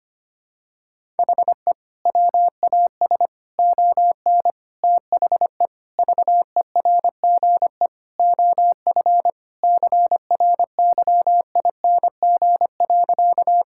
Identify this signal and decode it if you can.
Morse code, heard as a keyed tone, about 25 wpm, 700 Hz. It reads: HE WAS ON THE VERGE OF CRYING.
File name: practice / 25wpm / Sentences with 4-Word Phrases - 25wpm